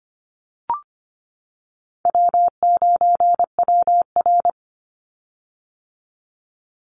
Morse code, heard as a keyed tone, about 25 wpm, 700 Hz. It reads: W9WR
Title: Easy Call Signs - 25wpm